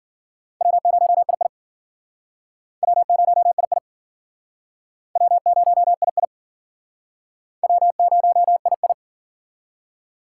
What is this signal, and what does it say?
Morse code, heard as a keyed tone, about 49 wpm, 700 Hz. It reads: W0II W0II W0II W0II
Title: Speed Racing - USA 1x2 Call Signs 40wpm